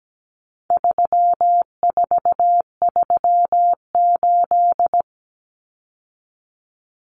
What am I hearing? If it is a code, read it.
Morse code, heard as a keyed tone, about 17 wpm, 700 Hz. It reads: 3438
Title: Numbers - 4-Digit 17wpm